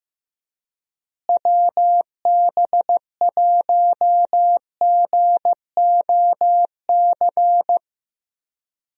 Morse code, heard as a keyed tone, about 15 wpm, 700 Hz. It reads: WB1GOC
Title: USA Call Signs - 6 Characters - Repeated 3x - 15wpm